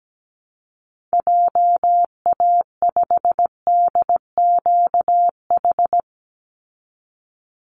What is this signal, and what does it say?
Morse code, heard as a keyed tone, about 17 wpm, 700 Hz. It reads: JA5DQH
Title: Call Signs Repeated 3x - 17wpm